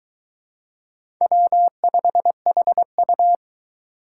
Morse code, heard as a keyed tone, about 22 wpm, 700 Hz. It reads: W5HU